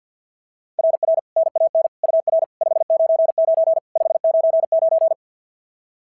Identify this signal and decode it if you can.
Morse code, heard as a keyed tone, about 52 wpm, 690 Hz. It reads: RR NAN UR 599 599
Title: SOTA QSO Practice 50wpm